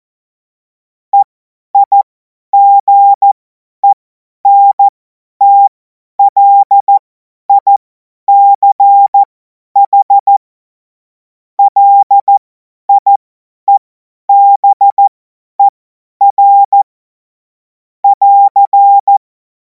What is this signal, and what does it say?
Morse code, heard as a keyed tone, about 14 wpm, spaced as for 10 wpm, 800 Hz. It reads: EIGENTLICH LIEBER <AR>
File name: Morsekurs / Witze / Witz 24. Februar 2026 50 F70